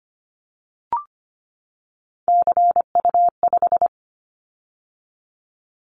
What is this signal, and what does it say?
Morse code, heard as a keyed tone, about 25 wpm, 700 Hz. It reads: CU5
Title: Callsign Prefixes for Top 100 DXCC - 25wpm